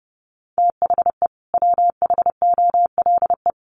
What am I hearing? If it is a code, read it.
Morse code, heard as a keyed tone, about 30 wpm, 700 Hz. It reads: THE WHOLE
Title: Sets of 2 Words 30wpm